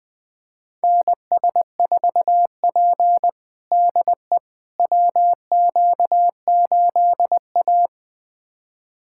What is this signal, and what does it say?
Morse code, heard as a keyed tone, about 20 wpm, 700 Hz. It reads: NS4P DE WQ8A
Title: USA Callsign DE Callsign - 4 Characters - 20wpm